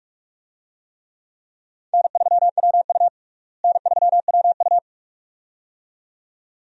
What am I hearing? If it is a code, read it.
Morse code, heard as a keyed tone, about 41 wpm, 700 Hz. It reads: N3WU N3WU
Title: USA Call Signs - 4 Characters - Repeated 2x - 45wpm